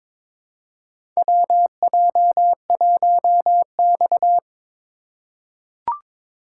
Morse code, heard as a keyed tone, about 22 wpm, 700 Hz. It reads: WJ1X
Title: USA Call Signs - 4 Characters - 2x Character Spacing - 22wpm